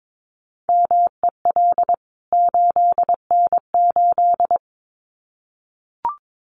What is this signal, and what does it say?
Morse code, heard as a keyed tone, about 22 wpm, 700 Hz. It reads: MEL 8N8
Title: CWT Exchange 22wpm